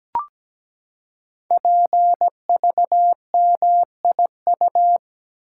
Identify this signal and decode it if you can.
Morse code, heard as a keyed tone, about 17 wpm, 700 Hz. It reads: PVMIU
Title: Code Groups - 5 Letter:Numbers - 17wpm